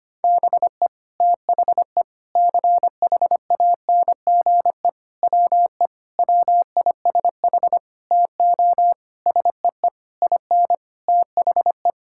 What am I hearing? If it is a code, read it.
Morse code, heard as a keyed tone, about 25 wpm, 700 Hz. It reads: BE THE CHANGE WE WISH TO SEE IN THE
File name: practice / 25wpm / Sentences with 9-Word Phrases - 25wpm